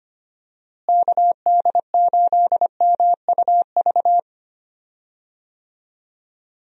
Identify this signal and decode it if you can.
Morse code, heard as a keyed tone, about 25 wpm, 700 Hz. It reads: KD8MUV
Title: USA Call Signs - 6 Characters - Repeated 3x - 25wpm